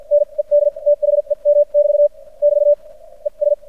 Morse code, heard as a keyed tone, about 54 wpm, spaced as for 41 wpm, 590 Hz. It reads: NED THEM <BT> <BT> ES
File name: archived / CRSnet